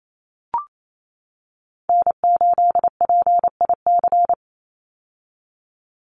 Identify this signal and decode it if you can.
Morse code, heard as a keyed tone, about 28 wpm, 700 Hz. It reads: N8PIC
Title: USA Call Signs 28wpm